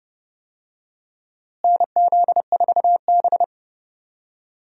Morse code, heard as a keyed tone, about 30 wpm, 700 Hz. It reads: NZ4B
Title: USA Call Signs - 4 Characters - Repeated 3x - 30wpm